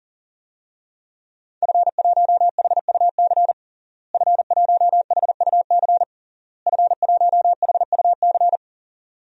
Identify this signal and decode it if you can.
Morse code, heard as a keyed tone, about 40 wpm, 700 Hz. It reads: F1HUC F1HUC F1HUC
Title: DXCC-100 Call Signs - 5 Characters - Repeated 3x - 40wpm